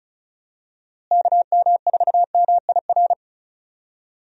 Morse code, heard as a keyed tone, about 36 wpm, 700 Hz. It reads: KM4MIR
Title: USA Call Signs - 6 Characters - Repeated 2x - 35wpm